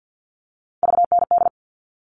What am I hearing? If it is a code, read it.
Morse code, heard as a keyed tone, about 50 wpm, 700 Hz. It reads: VND
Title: All Cut Numbers - 3-Digit 50wpm